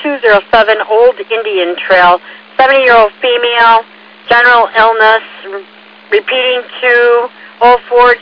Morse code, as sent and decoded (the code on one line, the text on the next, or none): none